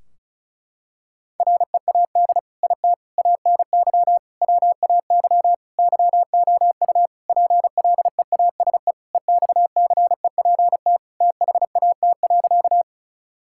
Morse code, heard as a keyed tone, about 36 wpm, 700 Hz. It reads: READ IT ANY WAY YOU PLEASE EXCEPT THAT.
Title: Sentences from Top 1600 Words 35wpm